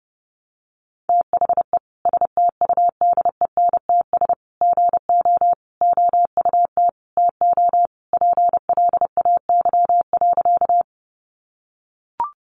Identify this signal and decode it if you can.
Morse code, heard as a keyed tone, about 30 wpm, 700 Hz. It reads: THE STUDENTS GO OUT TO PLAY.
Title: Easy Phrases 30wpm